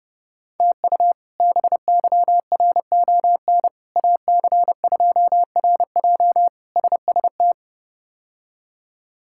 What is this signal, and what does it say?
Morse code, heard as a keyed tone, about 30 wpm, 700 Hz. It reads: TU BYRON AC2RJ SST
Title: K1USN SST QSO Practice - 30wpm